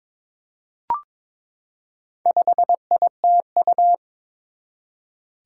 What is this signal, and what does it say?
Morse code, heard as a keyed tone, about 22 wpm, 700 Hz. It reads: 5ITU
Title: Four Challenging Letter-Number Combinations 22wpm